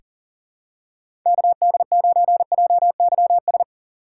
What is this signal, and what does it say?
Morse code, heard as a keyed tone, about 40 wpm, 700 Hz. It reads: KD9JYS
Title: USA Call Signs - 6 Characters - 4x Character Spacing - 40wpm